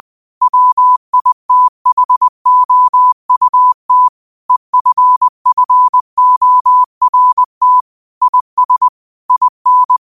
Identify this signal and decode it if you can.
Morse code, heard as a keyed tone, about 20 wpm, 1000 Hz. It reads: WITHOUT EFFORT IS IN